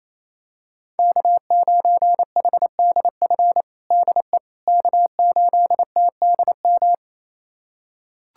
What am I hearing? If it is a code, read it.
Morse code, heard as a keyed tone, about 28 wpm, 700 Hz. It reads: K9HDF DE K8TDM
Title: USA Callsign DE Callsign - 5 Characters - 28wpm